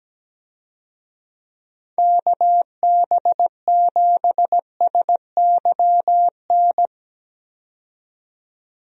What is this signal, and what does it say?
Morse code, heard as a keyed tone, about 17 wpm, 700 Hz. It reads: KB7SYN
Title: USA Call Signs - 6 Characters - 17wpm